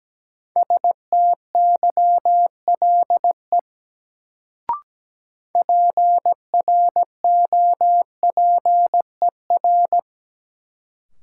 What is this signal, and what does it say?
Morse code, heard as a keyed tone, about 17 wpm, 700 Hz. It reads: STYLE PROPER
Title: Sentences from Top 800 Words - New Word Drill - 17wpm